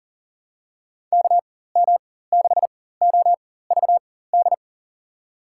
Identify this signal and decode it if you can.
Morse code, heard as a keyed tone, about 40 wpm, spaced as for 19 wpm, 700 Hz. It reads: KM6OVD
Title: USA Call Signs - 6 Characters - 4x Character Spacing - 40wpm